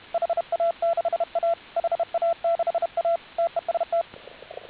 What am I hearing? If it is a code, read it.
Morse code, heard as a keyed tone, about 32 wpm, 690 Hz, with radio noise. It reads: HA6A HA6A TEST